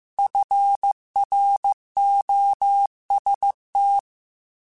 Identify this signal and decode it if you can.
Morse code, heard as a keyed tone, about 15 wpm, 800 Hz. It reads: FROST
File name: Weather Sets / Weather Sets-mp3s / Weather 4 24 @13